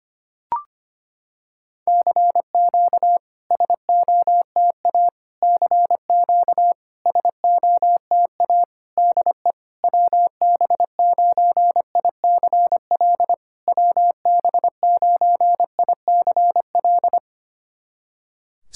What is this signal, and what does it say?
Morse code, heard as a keyed tone, about 25 wpm, 690 Hz. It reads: CQ SOTA CQ SOTA DE WB9ICL WB9ICL
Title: SOTA QSO Practice 25wpm